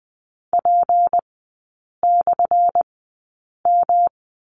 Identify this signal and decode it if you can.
Morse code, heard as a keyed tone, about 20 wpm, spaced as for 8 wpm, 700 Hz. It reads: P/M
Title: 191 Lesson 31e - Letter v - Three Characters 20wpm